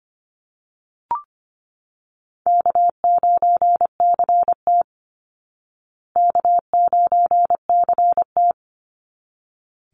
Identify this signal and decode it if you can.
Morse code, heard as a keyed tone, about 25 wpm, 700 Hz. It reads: K9CT K9CT
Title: Call Signs Repeated 3x - 25wpm